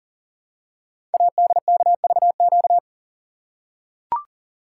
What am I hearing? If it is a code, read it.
Morse code, heard as a keyed tone, about 38 wpm, 690 Hz. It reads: ADKVQ
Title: Code Groups - 5 Letters - 40wpm